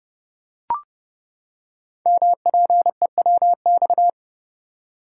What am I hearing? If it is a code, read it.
Morse code, heard as a keyed tone, about 30 wpm, 700 Hz. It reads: MPEWX
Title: Code Groups - 5 Letters - 30wpm